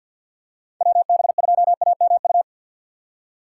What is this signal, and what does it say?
Morse code, heard as a keyed tone, about 49 wpm, 700 Hz. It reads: WB2AMU
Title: USA Call Signs - 4x Character Spacing - 50wpm